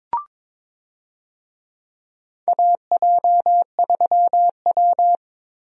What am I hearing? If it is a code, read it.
Morse code, heard as a keyed tone, about 22 wpm, 700 Hz. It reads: AJ3W
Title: USA Call Signs - 4 Characters - Repeated 2x - 22wpm